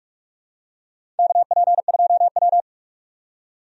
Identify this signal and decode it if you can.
Morse code, heard as a keyed tone, about 44 wpm, 700 Hz. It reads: KP2W
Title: USA Call Signs - 4 Characters - 45wpm